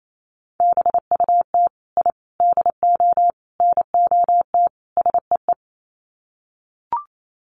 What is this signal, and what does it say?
Morse code, heard as a keyed tone, about 28 wpm, 700 Hz. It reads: BUT I DO NOT SEE
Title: Sets of 5 Words Encore 28wpm